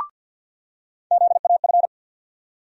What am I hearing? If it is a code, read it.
Morse code, heard as a keyed tone, about 52 wpm, 700 Hz. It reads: ZAF
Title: USA Call Sign Postfixes - 3 Letters 50wpm